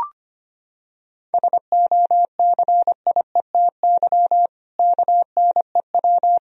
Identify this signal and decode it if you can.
Morse code, heard as a keyed tone, about 25 wpm, 700 Hz. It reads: SOCIETY KNEW
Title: Sentences from Top 1000 Words - Review Part 4 - 25wpm